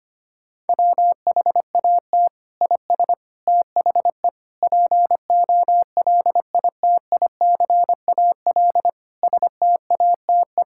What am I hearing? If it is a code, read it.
Morse code, heard as a keyed tone, about 25 wpm, 700 Hz. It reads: WHAT IS THE POLITICAL STATE
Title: Sentences from Top 1000 Words - Review Part 5 - 25wpm